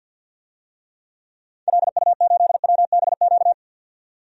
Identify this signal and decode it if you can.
Morse code, heard as a keyed tone, about 51 wpm, 700 Hz. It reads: LU8WBQ
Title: DXCC-100 Call Signs - 6 Characters - 50wpm